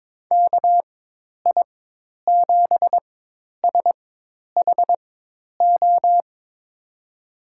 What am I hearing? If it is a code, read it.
Morse code, heard as a keyed tone, about 22 wpm, spaced as for 10 wpm, 700 Hz. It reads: KI7SHO
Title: USA Call Signs - 6 Characters - 4x Character Spacing - 22wpm